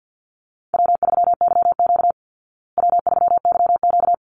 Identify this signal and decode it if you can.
Morse code, heard as a keyed tone, about 49 wpm, 700 Hz. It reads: W3YQ W3YQ
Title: Easy Call Signs Repeated 2x - 50wpm